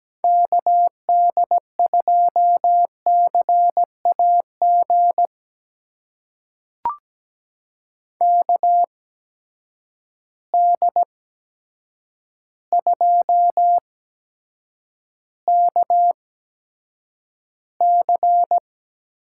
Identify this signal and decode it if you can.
Morse code, heard as a keyed tone, about 17 wpm, spaced as for 8 wpm, 700 Hz. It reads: KD2CAG KD2KC
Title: USA Call Signs - 6 Characters - 8x Character Spacing - 17wpm